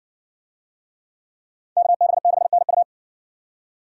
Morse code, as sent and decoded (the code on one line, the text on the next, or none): -.- -... -.... -. ..-
KB6NU